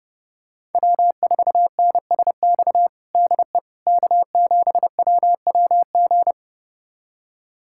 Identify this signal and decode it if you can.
Morse code, heard as a keyed tone, about 30 wpm, 700 Hz. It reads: W4NSX DE K7WWG
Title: USA Callsign DE Callsign 30wpm